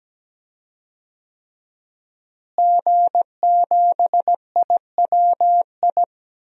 Morse code, heard as a keyed tone, about 17 wpm, 700 Hz. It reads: G7IWI
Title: DXCC-100 Call Signs - 5 Characters - Repeated 3x - 17wpm